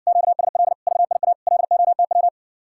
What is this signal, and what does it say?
Morse code, heard as a keyed tone, about 60 wpm, 700 Hz. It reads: QSL VIA LOTW